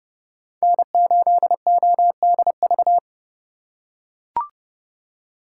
Morse code, heard as a keyed tone, about 30 wpm, 700 Hz. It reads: N8ODV
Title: USA Call Signs - 2x Character Spacing - 30wpm